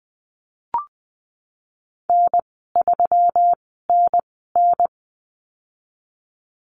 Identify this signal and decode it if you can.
Morse code, heard as a keyed tone, about 20 wpm, spaced as for 15 wpm, 700 Hz. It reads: N3NN
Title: 167 Lesson 27g - Number 7 - Call Signs 20wpm